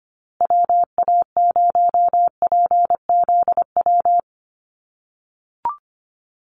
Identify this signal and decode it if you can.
Morse code, heard as a keyed tone, about 25 wpm, 700 Hz. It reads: WA0PZW
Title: USA Call Signs Repeated 3x 25wpm